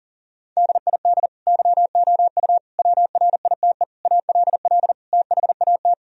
Morse code, heard as a keyed tone, about 40 wpm, 700 Hz. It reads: DID YOU WRITE ALL THAT
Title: Sentences from Top 1000 Words - Review Part 1 - 40wpm